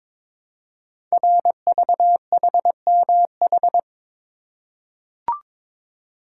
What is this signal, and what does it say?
Morse code, heard as a keyed tone, about 22 wpm, 700 Hz. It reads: RVHMH
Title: Code Groups - 5 Letters - 22wpm